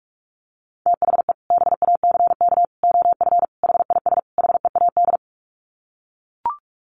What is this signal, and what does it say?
Morse code, heard as a keyed tone, about 44 wpm, 700 Hz. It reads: THE BACK OF HIS HEAD
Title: Sets of 5 Words Encore 45wpm